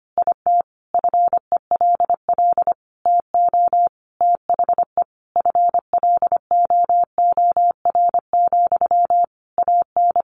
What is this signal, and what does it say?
Morse code, heard as a keyed tone, about 25 wpm, 700 Hz. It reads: IT FELL TO THE FLOOR, AN
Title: A Sound Of Thunder 25wpm